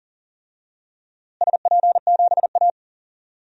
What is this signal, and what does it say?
Morse code, heard as a keyed tone, about 40 wpm, 700 Hz. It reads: SP7A
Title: DXCC-100 Call Signs - 4 Characters - Repeated 2x - 40wpm